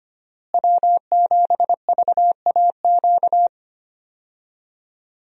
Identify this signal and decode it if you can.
Morse code, heard as a keyed tone, about 25 wpm, 700 Hz. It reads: W7VAQ
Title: USA Call Signs - 5 Characters - Repeated 2x - 25wpm